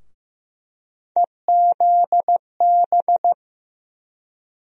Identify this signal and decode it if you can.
Morse code, heard as a keyed tone, about 15 wpm, 700 Hz. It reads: EZB
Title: USA Call Sign Postfixes - 3 Letters 15wpm